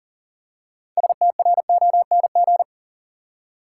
Morse code, heard as a keyed tone, about 40 wpm, 700 Hz. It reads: STRONG
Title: Six Letter Words - 40wpm